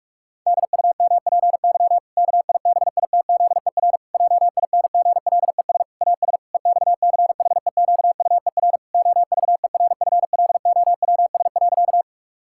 Morse code, heard as a keyed tone, about 46 wpm, 700 Hz. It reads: DUMPY KIBITZER JINGLES AS EXCHEQUER OVERFLOWS.
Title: Pangram Challenge v2 - 45wpm